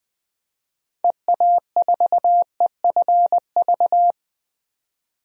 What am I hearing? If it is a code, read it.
Morse code, heard as a keyed tone, about 20 wpm, 700 Hz. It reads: EA4EFV
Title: DXCC-100 Call Signs - 6 Characters - 20wpm